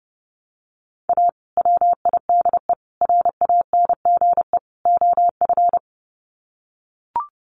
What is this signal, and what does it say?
Morse code, heard as a keyed tone, about 30 wpm, 700 Hz. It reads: A WIDE RANGE OF
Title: Sets of 4 Words 30wpm